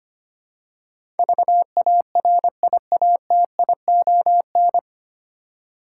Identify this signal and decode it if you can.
Morse code, heard as a keyed tone, about 25 wpm, 700 Hz. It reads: VARIATION
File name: practice / 25wpm / 9 Letter Words - First 3 Letters Given - 25wpm